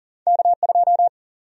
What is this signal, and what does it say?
Morse code, heard as a keyed tone, about 40 wpm, 700 Hz. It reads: K2
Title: USA Call Sign Prefixes - Repeated 2x - 40wpm